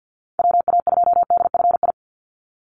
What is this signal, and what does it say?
Morse code, heard as a keyed tone, about 50 wpm, 700 Hz. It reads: WA2DFI
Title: Call Signs Repeated 2x - 50wpm